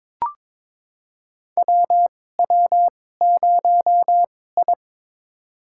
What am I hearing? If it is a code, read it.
Morse code, heard as a keyed tone, about 22 wpm, spaced as for 16 wpm, 700 Hz. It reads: WW0I
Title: USA Call Signs - 4 Characters - 2x Character Spacing - 22wpm